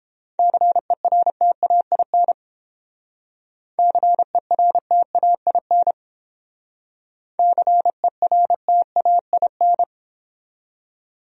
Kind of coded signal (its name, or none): Morse code